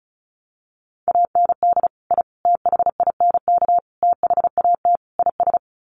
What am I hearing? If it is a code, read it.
Morse code, heard as a keyed tone, about 34 wpm, 700 Hz. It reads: AND I THINK THAT IS
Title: Sets of 5 Words Encore2 35wpm